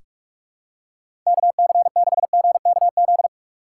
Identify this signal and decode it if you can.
Morse code, heard as a keyed tone, about 47 wpm, 700 Hz. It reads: KC6GKZ
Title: USA Call Signs - 6 Characters - 6x Character Spacing - 45wpm